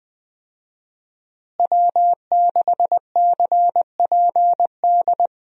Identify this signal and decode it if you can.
Morse code, heard as a keyed tone, about 21 wpm, 700 Hz. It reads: W6CPD
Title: USA Call Signs - 5 Characters - Repeated 3x - 20wpm